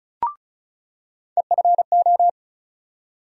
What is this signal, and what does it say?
Morse code, heard as a keyed tone, about 36 wpm, 700 Hz. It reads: EFO
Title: USA Call Sign Postfixes - 3 Letters 35wpm